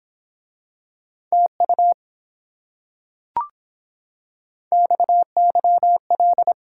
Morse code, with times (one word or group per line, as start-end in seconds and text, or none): 1.3-1.9 TU
4.7-6.5 XYL